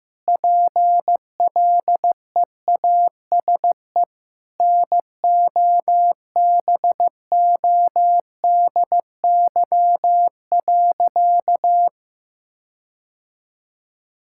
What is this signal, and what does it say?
Morse code, heard as a keyed tone, about 15 wpm, 700 Hz. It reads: PLEASE NOBODY.
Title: Sentences from Top 1600 Words 15wpm